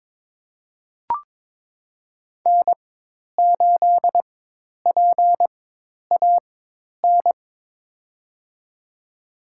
Morse code, heard as a keyed tone, about 22 wpm, spaced as for 10 wpm, 700 Hz. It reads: N8PAN
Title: USA Call Signs - 5 Characters - 4x Character Spacing - 22wpm